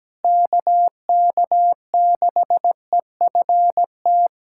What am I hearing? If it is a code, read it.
Morse code, heard as a keyed tone, about 17 wpm, 700 Hz. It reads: KK6EFT
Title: USA Callsign DE Callsign - 6 Characters - 17wpm